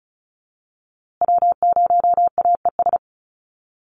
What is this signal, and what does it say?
Morse code, heard as a keyed tone, about 34 wpm, 700 Hz. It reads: W0AES
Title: USA Call Signs Repeated 3x 35wpm